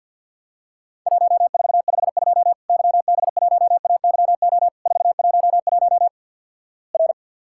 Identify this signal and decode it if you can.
Morse code, heard as a keyed tone, about 49 wpm, 700 Hz. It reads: 1452 YB1AYO 411 R
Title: Worked All Europe DX Contest - QTC Exchange - 50wpm